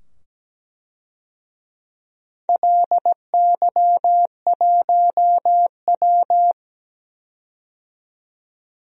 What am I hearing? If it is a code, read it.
Morse code, heard as a keyed tone, about 17 wpm, 700 Hz. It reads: LY1W
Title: DXCC-100 Call Signs - 4 Characters - 17wpm